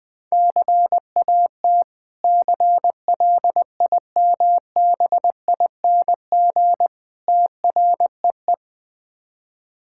Morse code, heard as a keyed tone, about 20 wpm, 690 Hz. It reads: CAT CLIMBING TREE
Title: Difficult Words with Hints - 20wpm